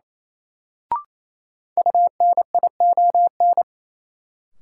Morse code, heard as a keyed tone, about 28 wpm, 700 Hz. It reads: UNION
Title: Sentences from Top 1400 Words - New Word Drill Alt - 28wpm